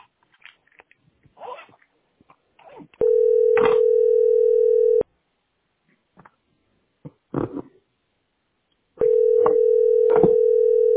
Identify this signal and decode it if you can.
Ringback tone (North America), 440 + 480 Hz.
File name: 399d2f76-4980-42a6-8abf